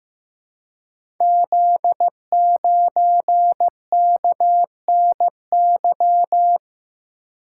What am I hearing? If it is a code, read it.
Morse code, heard as a keyed tone, about 15 wpm, 700 Hz. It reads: Z9KNY
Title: Code Groups - 5 Letter:Numbers - 15wpm